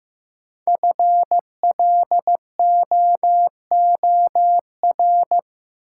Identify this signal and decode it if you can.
Morse code, heard as a keyed tone, about 15 wpm, 700 Hz. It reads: FLOOR
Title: Sets of 3 Words - New Word Drill - 15wpm